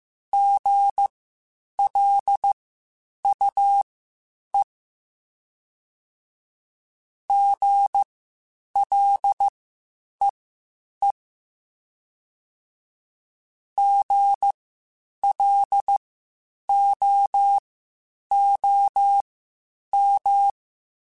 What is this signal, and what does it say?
Morse code, heard as a keyed tone, about 15 wpm, spaced as for 8 wpm, 800 Hz. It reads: GLUE GLEE GLOOM
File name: Spaced Alphas / Spaced Alphas mp3s / Gs sp @13